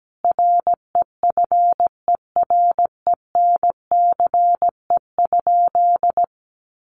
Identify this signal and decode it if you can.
Morse code, heard as a keyed tone, about 17 wpm, 700 Hz. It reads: REFERENCE?
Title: Sentences from Top 1200 Words 17wpm